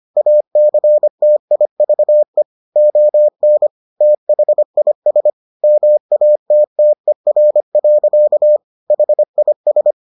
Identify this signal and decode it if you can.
Morse code, heard as a keyed tone, about 25 wpm, 600 Hz. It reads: ACTIVE ON THIS MATTER. HIS